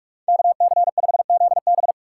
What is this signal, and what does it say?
Morse code, heard as a keyed tone, about 46 wpm, 700 Hz. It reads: KC5ZB
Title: USA Call Signs - 3x Character Spacing - 45wpm